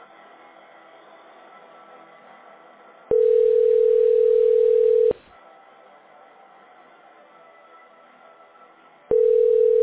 Ringback tone (North America), 440 + 480 Hz.